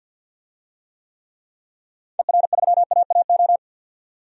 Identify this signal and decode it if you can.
Morse code, heard as a keyed tone, about 50 wpm, 700 Hz. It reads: ER3AAO
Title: DXCC-100 Call Signs - 6 Characters - 50wpm